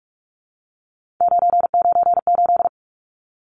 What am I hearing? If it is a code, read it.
Morse code, heard as a keyed tone, about 44 wpm, 700 Hz. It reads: 998
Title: Numbers - 3-Digit 45wpm